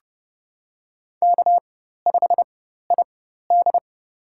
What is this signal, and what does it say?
Morse code, heard as a keyed tone, about 30 wpm, spaced as for 14 wpm, 700 Hz. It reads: K5ID
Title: USA Call Signs - 4 Characters - 4x Character Spacing - 30wpm